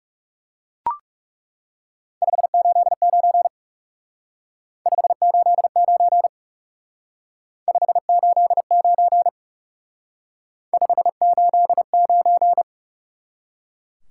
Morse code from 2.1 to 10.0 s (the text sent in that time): 589 589 589